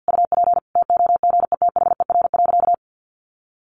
Morse code, heard as a keyed tone, about 49 wpm, 700 Hz. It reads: UP TOGETHER.
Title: Sentences from Top 1500 Words 50wpm